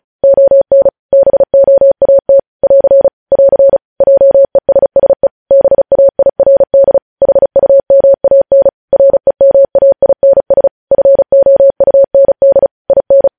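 Morse code, heard as a keyed tone, about 36 wpm, 550 Hz. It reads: ON BOAT <AR> <AR> JESSE BAIRD HUMAN REMAINS FOUND IN
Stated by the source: Morse code practice reading news headlines at 35 words per minute.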